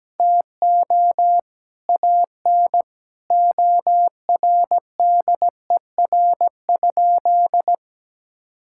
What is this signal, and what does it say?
Morse code, heard as a keyed tone, about 17 wpm, 700 Hz. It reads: TO AN ORDER?
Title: Sentences from Top 1600 Words 17wpm